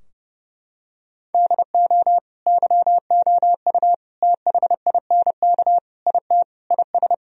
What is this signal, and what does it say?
Morse code, heard as a keyed tone, about 32 wpm, 700 Hz. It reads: DO YOU THINK IT IS
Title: Sentences from Top 1000 Words - Review Part 4 - 30wpm